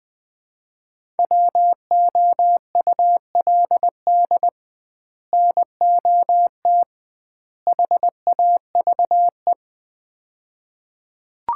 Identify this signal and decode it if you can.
Morse code, heard as a keyed tone, about 21 wpm, 700 Hz. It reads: WOULD NOT HAVE
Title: 2x Word-Spacing - Sets of 3 Words - 20wpm